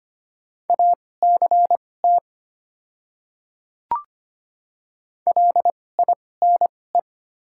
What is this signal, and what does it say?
Morse code, heard as a keyed tone, about 25 wpm, spaced as for 18 wpm, 700 Hz. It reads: ACT LINE
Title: 2x Character Spacing - Top 300 Words - 25wpm